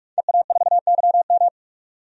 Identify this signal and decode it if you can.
Morse code, heard as a keyed tone, about 45 wpm, 700 Hz. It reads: EA4YM
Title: DXCC-100 Call Signs - 5 Characters - 45wpm